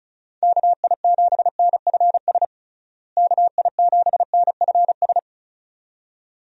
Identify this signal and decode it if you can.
Morse code, heard as a keyed tone, about 34 wpm, 700 Hz. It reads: KI7NFS KI7NFS
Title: USA Call Signs - 6 Characters - Repeated 2x - 35wpm